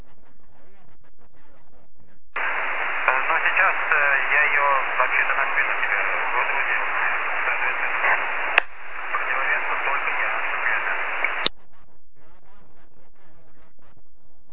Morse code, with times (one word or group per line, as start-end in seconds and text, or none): none